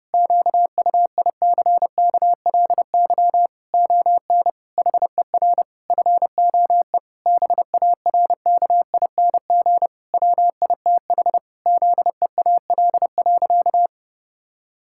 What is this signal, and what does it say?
Morse code, heard as a keyed tone, about 32 wpm, 700 Hz. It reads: QUICKLY ON HER FOE BARKING WITH ZEAL.
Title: Pangram Challenge v2 - 30wpm